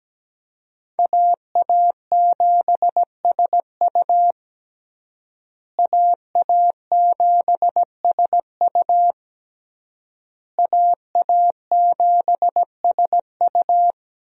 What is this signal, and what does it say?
Morse code, heard as a keyed tone, about 17 wpm, 700 Hz. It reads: AA7SU AA7SU AA7SU
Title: USA Call Signs - 5 Characters - Repeated 3x - 17wpm